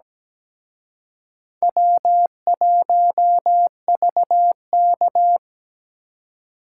Morse code, heard as a keyed tone, about 20 wpm, 700 Hz. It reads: W1VK
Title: Speed Racing - USA 1x2 Call Signs 17wpm